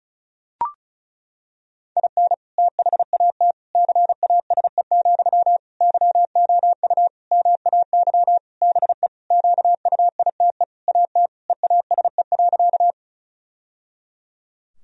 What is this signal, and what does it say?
Morse code, heard as a keyed tone, about 36 wpm, 700 Hz. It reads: IN THAT CASE, YOU MAY BE QUITE AT EASE.
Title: Sentences from Top 1000 Words - Review Part 1 - 35wpm